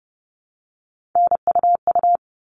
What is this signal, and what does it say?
Morse code, heard as a keyed tone, about 30 wpm, 700 Hz. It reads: NUU
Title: All Cut Numbers - 3-Digit 30wpm